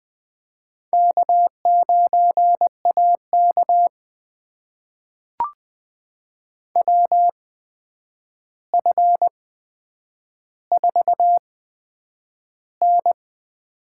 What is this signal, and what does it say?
Morse code, heard as a keyed tone, about 20 wpm, spaced as for 9 wpm, 700 Hz. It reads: K9AK WF4N
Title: USA Call Signs - 4 Characters - 8x Character Spacing - 20wpm